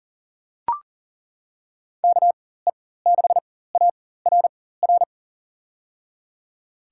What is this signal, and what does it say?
Morse code, heard as a keyed tone, about 40 wpm, spaced as for 19 wpm, 700 Hz. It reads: KE6ARR
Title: USA Call Signs - 6 Characters - 4x Character Spacing - 40wpm